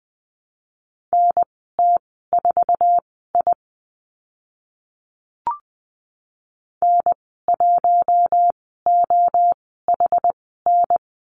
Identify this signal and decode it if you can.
Morse code, heard as a keyed tone, about 20 wpm, spaced as for 14 wpm, 700 Hz. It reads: NT4I N1OHN
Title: 062 Lesson 11g - Letter h - Call Signs 20wpm